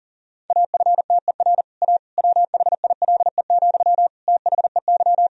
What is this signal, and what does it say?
Morse code, heard as a keyed tone, about 40 wpm, 700 Hz. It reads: AFTER A WHILE, THEY
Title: Sentences from Top 1000 Words - Review Part 1 - 40wpm